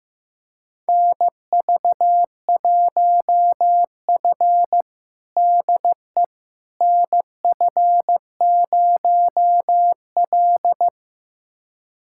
Morse code, heard as a keyed tone, about 15 wpm, 700 Hz. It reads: NV1F DE NF0L
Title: USA Callsign DE Callsign - 4 Characters - 15wpm